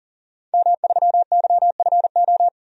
Morse code, heard as a keyed tone, about 40 wpm, 700 Hz. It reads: M3YFO
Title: DXCC-100 Call Signs - 5 Characters - 40wpm